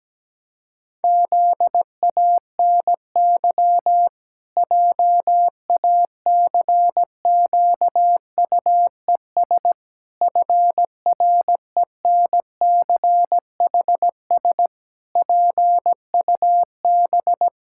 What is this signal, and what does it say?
Morse code, heard as a keyed tone, about 17 wpm, 700 Hz. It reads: ZANY JACQUES FRENCHS PUB
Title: Pangram Challenge v2 - 17wpm